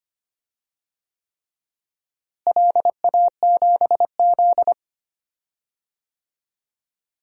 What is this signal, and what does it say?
Morse code, heard as a keyed tone, about 25 wpm, 700 Hz. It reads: LA7Z
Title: DXCC-100 Call Signs - 4 Characters - Repeated 3x - 25wpm